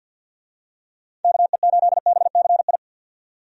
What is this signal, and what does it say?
Morse code, heard as a keyed tone, about 50 wpm, 700 Hz. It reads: KE8BCI
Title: USA Call Signs - 6 Characters - 50wpm